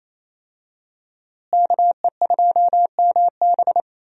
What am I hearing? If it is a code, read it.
Morse code, heard as a keyed tone, about 28 wpm, 700 Hz. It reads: KE2MB
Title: USA Call Signs - 5 Characters - Repeated 2x - 28wpm